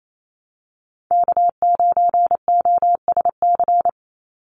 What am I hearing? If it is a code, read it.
Morse code, heard as a keyed tone, about 28 wpm, 700 Hz. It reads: K9OSC
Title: NA SOTA Chaser Callsigns 28wpm